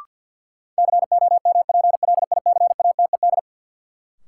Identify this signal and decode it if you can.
Morse code, heard as a keyed tone, about 53 wpm, 700 Hz. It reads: COMPLICATED
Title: Sentences from Top 1200 Words - New Word Drill Alt - 50wpm